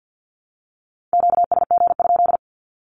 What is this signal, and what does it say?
Morse code, heard as a keyed tone, about 48 wpm, 700 Hz. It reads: QSG?
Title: Q-Codes 50wpm